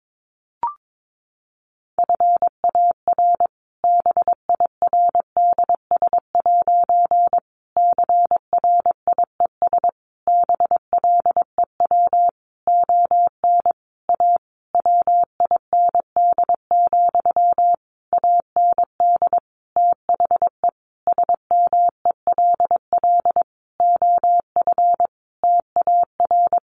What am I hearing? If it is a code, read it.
Morse code, heard as a keyed tone, about 22 wpm, 700 Hz. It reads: FAR BIRDS' CRIES BLEW ON A WIND, AND THE SMELL OF TAR
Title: A Sound Of Thunder 22wpm